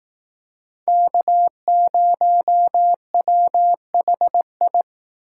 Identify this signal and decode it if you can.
Morse code, heard as a keyed tone, about 18 wpm, 700 Hz. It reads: K0WHI
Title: Speed Racing - USA 1x3 Call Signs 15wpm